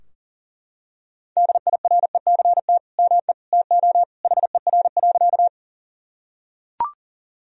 Morse code, heard as a keyed tone, about 40 wpm, 700 Hz. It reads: DIRECT ME TO HER.
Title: Sentences from Top 1000 Words - Review Part 2 - 40wpm